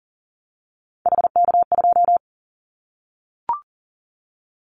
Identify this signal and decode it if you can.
Morse code, heard as a keyed tone, about 40 wpm, 700 Hz. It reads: HK2
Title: Callsign Prefixes for Top 100 DXCC - 40wpm